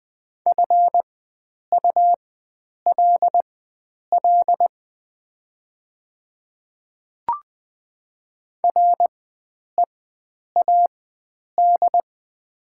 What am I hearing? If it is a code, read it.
Morse code, heard as a keyed tone, about 20 wpm, spaced as for 9 wpm, 700 Hz. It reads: FULL READ
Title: 4x Character Spacing - Top 500 Words - 20wpm